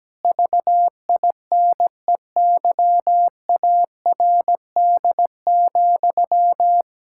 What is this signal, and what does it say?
Morse code, heard as a keyed tone, about 17 wpm, 700 Hz. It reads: VINEYARD,
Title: Pangram Challenge v2 - 17wpm